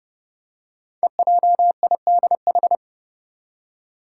Morse code, heard as a keyed tone, about 30 wpm, 690 Hz. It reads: EJIDH
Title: Code Groups - 5 Letters - 30wpm